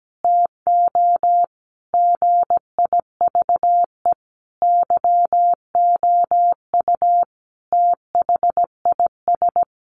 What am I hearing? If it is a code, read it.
Morse code, heard as a keyed tone, about 17 wpm, 700 Hz. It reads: TO GIVE YOU THIS
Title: Sentences from Top 1200 Words 17wpm